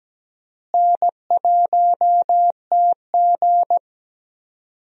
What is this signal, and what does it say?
Morse code, heard as a keyed tone, about 17 wpm, 700 Hz. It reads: N1TG
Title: USA Call Signs - 4 Characters - Repeated 3x - 17wpm